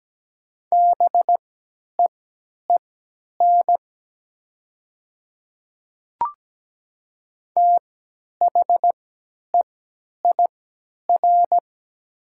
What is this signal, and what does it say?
Morse code, heard as a keyed tone, about 17 wpm, spaced as for 10 wpm, 700 Hz. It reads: BEEN THEIR
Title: 3x Character Spacing - Top 200 Words - 17wpm